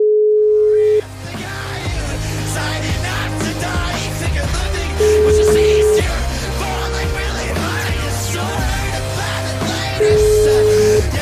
Ringback tone (Europe, 425 Hz), with noise.